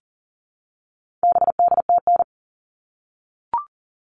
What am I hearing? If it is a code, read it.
Morse code, heard as a keyed tone, about 40 wpm, 700 Hz. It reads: BDTN
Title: All Cut Numbers - 4-Digit 40wpm